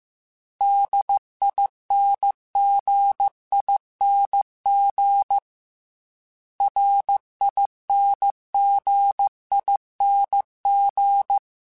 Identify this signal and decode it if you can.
Morse code, heard as a keyed tone, about 15 wpm, 800 Hz. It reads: DINGING RINGING
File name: Special INGs / Special INGs-mp3s / Dble Ings @13